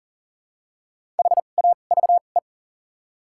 Morse code, heard as a keyed tone, about 40 wpm, spaced as for 29 wpm, 700 Hz. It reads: HAVE
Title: 2x Character Spacing - Top 100 Words - 40wpm